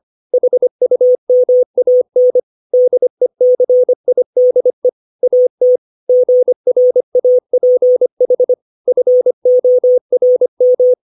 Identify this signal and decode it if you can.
Morse code, heard as a keyed tone, about 25 wpm, 500 Hz. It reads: HUMAN DECIDE AT GRAPH FORM
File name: 632 words / 632 Words 25WPM